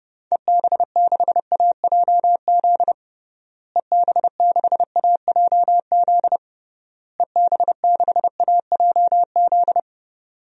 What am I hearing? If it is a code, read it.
Morse code, heard as a keyed tone, about 30 wpm, 700 Hz. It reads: EB6AJZ EB6AJZ EB6AJZ
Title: DXCC-100 Call Signs - 6 Characters - Repeated 3x - 30wpm